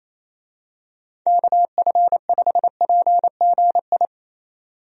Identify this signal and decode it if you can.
Morse code, heard as a keyed tone, about 28 wpm, 700 Hz. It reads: KF5PGI
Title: USA Call Signs - 6 Characters - 28wpm